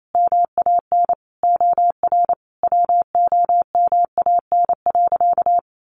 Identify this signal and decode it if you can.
Morse code, heard as a keyed tone, about 28 wpm, 700 Hz. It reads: MAN OR WOMAN.
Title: Sentences from Top 1400 Words 28wpm